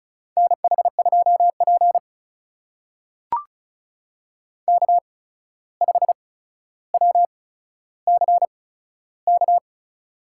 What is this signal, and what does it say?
Morse code, heard as a keyed tone, about 34 wpm, spaced as for 14 wpm, 700 Hz. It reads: NH2P K5WCK
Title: USA Call Signs - 8x Character Spacing - 35wpm